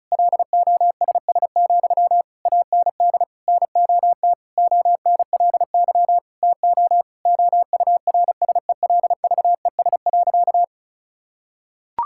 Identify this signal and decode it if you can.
Morse code, heard as a keyed tone, about 34 wpm, 700 Hz. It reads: LOSS, AND NOT ONLY TO OURSELVES.
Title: Sentences from Top 1600 Words 35wpm